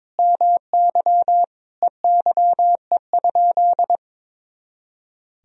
Morse code, heard as a keyed tone, about 22 wpm, 700 Hz. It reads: MY EYE?
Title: Sentences from Top 500 Words and Abbreviations 22wpm